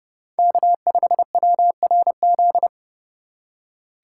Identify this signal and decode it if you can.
Morse code, heard as a keyed tone, about 30 wpm, 700 Hz. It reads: K5WRZ
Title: USA Call Signs - 8x Character Spacing - 30wpm